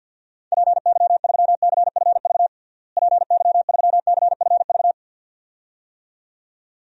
Morse code, heard as a keyed tone, about 50 wpm, 700 Hz. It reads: PY3CFV PY3CFV
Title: DXCC-100 Call Signs - 6 Characters - Repeated 2x - 50wpm